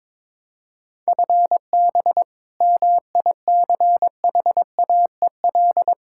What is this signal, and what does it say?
Morse code, heard as a keyed tone, about 21 wpm, 700 Hz. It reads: FB MICHAEL
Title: Practice QSO - Exchange 3 - More Info - 22wpm